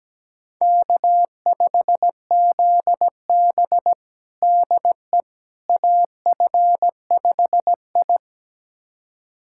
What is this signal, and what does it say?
Morse code, heard as a keyed tone, about 17 wpm, 700 Hz. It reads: K5ZB DE AF5I
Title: USA Callsign DE Callsign - 4 Characters - 17wpm